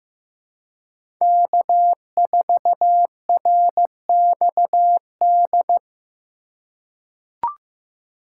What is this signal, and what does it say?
Morse code, heard as a keyed tone, about 15 wpm, 700 Hz. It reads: K4RXD
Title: USA Call Signs - 5 Characters - 2x Character Spacing - 15wpm